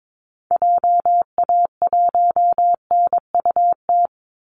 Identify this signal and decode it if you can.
Morse code, heard as a keyed tone, about 22 wpm, 700 Hz. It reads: JA1NUT
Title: Call Signs Repeated 2x - 22wpm